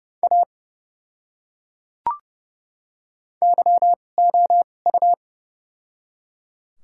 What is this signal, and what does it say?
Morse code, heard as a keyed tone, about 29 wpm, spaced as for 21 wpm, 700 Hz. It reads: A YOU
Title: 2x Character Spacing - Top 100 Words - 30wpm